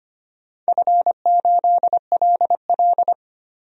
Morse code, heard as a keyed tone, about 25 wpm, 700 Hz. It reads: F8LL
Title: DXCC-100 Call Signs - 4 Characters - 25wpm